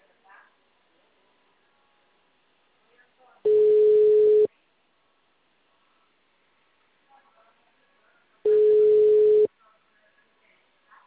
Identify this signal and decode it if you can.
Ringback tone (Europe, 425 Hz), with speech.
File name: out- -2019- - - .405